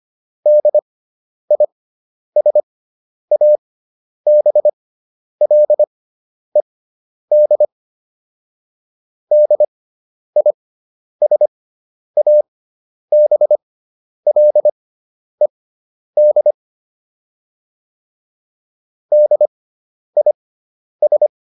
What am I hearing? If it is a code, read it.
Morse code, heard as a keyed tone, about 25 wpm, spaced as for 10 wpm, 600 Hz. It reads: DISABLED DISABLED DIS